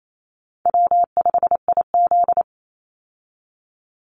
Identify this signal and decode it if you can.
Morse code, heard as a keyed tone, about 28 wpm, 700 Hz. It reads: W5IZ
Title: Easy Call Signs - 28wpm